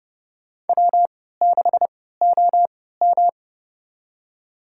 Morse code, heard as a keyed tone, about 30 wpm, spaced as for 17 wpm, 700 Hz. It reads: W6OM
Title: USA Call Signs - 4 Characters - 3x Character Spacing - 30wpm